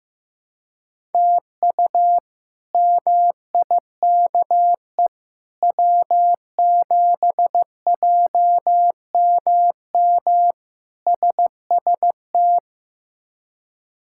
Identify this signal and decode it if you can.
Morse code, heard as a keyed tone, about 15 wpm, 700 Hz. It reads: TU MIKE W7JMM SST
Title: K1USN SST QSO Practice - 15wpm